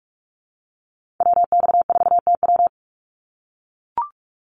Morse code, heard as a keyed tone, about 44 wpm, 700 Hz. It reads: WX4TW
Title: NA SOTA Activator Callsigns 45wpm